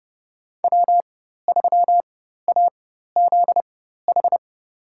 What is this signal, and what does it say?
Morse code, heard as a keyed tone, about 30 wpm, spaced as for 14 wpm, 700 Hz. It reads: W3AZH
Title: USA Call Signs - 5 Characters - 4x Character Spacing - 30wpm